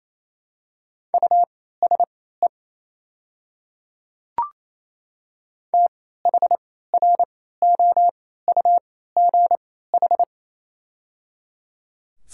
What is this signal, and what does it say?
Morse code, heard as a keyed tone, about 28 wpm, spaced as for 16 wpm, 690 Hz. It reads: USE THROUGH
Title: 3x Character Spacing - Top 500 Words - 28wpm